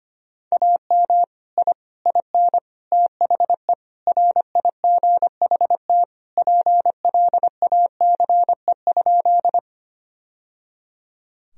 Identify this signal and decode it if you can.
Morse code, heard as a keyed tone, about 25 wpm, 700 Hz. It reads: AM I IN THE RIGHT PLACE?
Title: Sentences from Top 1000 Words - Review Part 2 - 25wpm